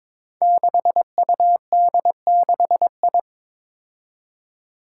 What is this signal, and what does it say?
Morse code, heard as a keyed tone, about 22 wpm, 700 Hz. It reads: 6UD6I
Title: Five Challenging Letter-Number Combinations 22wpm